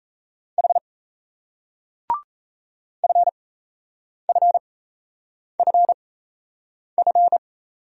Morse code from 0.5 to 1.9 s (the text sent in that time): H